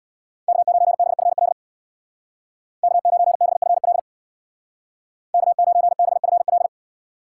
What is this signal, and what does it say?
Morse code, heard as a keyed tone, about 64 wpm, 700 Hz. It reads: K9BFL K9BFL K9BFL